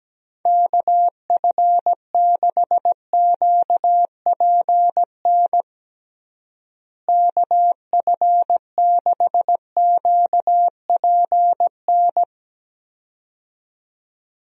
Morse code, heard as a keyed tone, about 17 wpm, 700 Hz. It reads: KF6QPN KF6QPN
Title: USA Call Signs - 6 Characters - Repeated 2x - 17wpm